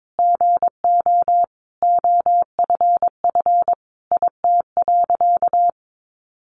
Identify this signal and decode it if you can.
Morse code, heard as a keyed tone, about 22 wpm, 700 Hz. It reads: GO OFF IT.
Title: A Sound Of Thunder 22wpm